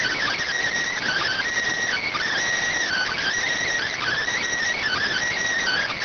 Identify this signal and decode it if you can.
Emergency Alert System (SAME) tones (header bursts), heard as harsh buzzing bursts. It is heard from 0.9 to 5.3 s.